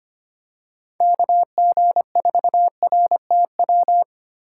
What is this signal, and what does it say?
Morse code, heard as a keyed tone, about 25 wpm, 700 Hz. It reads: KG4RTW
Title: USA Call Signs - 6 Characters - 2x Character Spacing - 25wpm